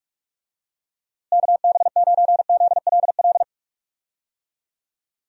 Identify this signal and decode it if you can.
Morse code, heard as a keyed tone, about 46 wpm, 700 Hz. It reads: KB9ZLL
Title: USA Call Signs - 6 Characters - Repeated 3x - 45wpm